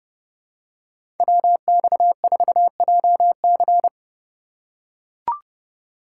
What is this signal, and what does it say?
Morse code, heard as a keyed tone, about 30 wpm, 700 Hz. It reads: WX4JC
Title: USA Call Signs - 5 Characters - 4x Character Spacing - 30wpm